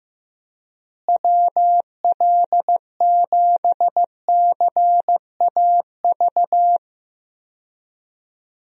Morse code, heard as a keyed tone, about 15 wpm, 700 Hz. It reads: WL7CAV
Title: USA Call Signs - 6 Characters - Repeated 3x - 15wpm